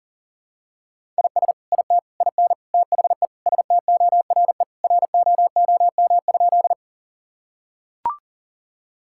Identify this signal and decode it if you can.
Morse code, heard as a keyed tone, about 37 wpm, 700 Hz. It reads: IS IT IN THE STORE ROOM?
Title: Sentences from Top 1000 Words - Review Part 5 - 40wpm